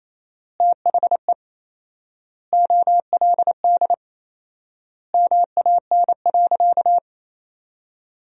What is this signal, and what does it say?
Morse code, heard as a keyed tone, about 28 wpm, 700 Hz. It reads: THE OLD MAN.
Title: 4x Word-Spacing - Sentences from Top 200 Words - 28wpm